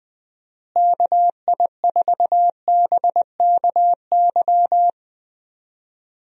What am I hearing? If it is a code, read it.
Morse code, heard as a keyed tone, about 21 wpm, 700 Hz. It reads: KI4BKY
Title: USA Call Signs - 6 Characters - Repeated 2x - 20wpm